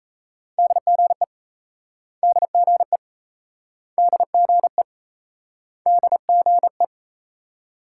Morse code, heard as a keyed tone, about 37 wpm, 700 Hz. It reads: DGE DGE DGE DGE